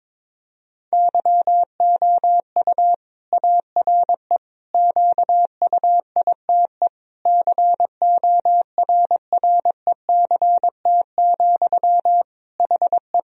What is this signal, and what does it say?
Morse code, heard as a keyed tone, about 22 wpm, 700 Hz. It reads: YOU ARE QUITE CORRECT, HE
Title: Sentences from Top 1000 Words - Review Part 2 - 22wpm